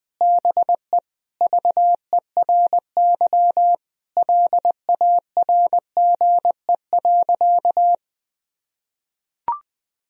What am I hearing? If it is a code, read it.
Morse code, heard as a keyed tone, about 20 wpm, 700 Hz. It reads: BE VERY LARGE.
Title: Sentences from Top 1000 Words - Review Part 1 - 20wpm